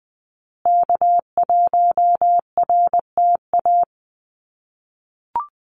Morse code, heard as a keyed tone, about 20 wpm, 700 Hz. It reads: K1RTA
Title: USA Call Signs Repeated 2x 20wpm